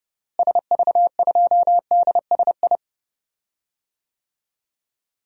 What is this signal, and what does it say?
Morse code, heard as a keyed tone, about 30 wpm, 700 Hz. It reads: SV2DSI
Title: DXCC-100 Call Signs - 6 Characters - Repeated 3x - 30wpm